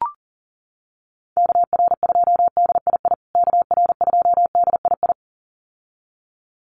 Morse code, heard as a keyed tone, about 40 wpm, 700 Hz. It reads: KR2DII KR2DII
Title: USA Call Signs Repeated 2x 40wpm